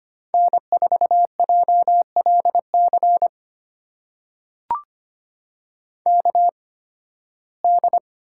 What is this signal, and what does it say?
Morse code, heard as a keyed tone, about 25 wpm, spaced as for 16 wpm, 700 Hz. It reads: N4JLC KD
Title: USA Call Signs - 5 Characters - 8x Character Spacing - 25wpm